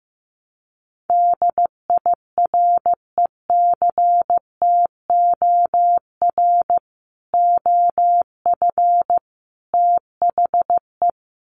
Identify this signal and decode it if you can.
Morse code, heard as a keyed tone, about 15 wpm, 700 Hz. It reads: DIRECTOR OF THE
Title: Sets of 3 Words 15wpm